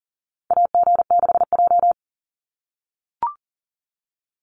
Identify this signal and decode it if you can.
Morse code, heard as a keyed tone, about 40 wpm, 700 Hz. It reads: AG6J
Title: USA Call Signs 40wpm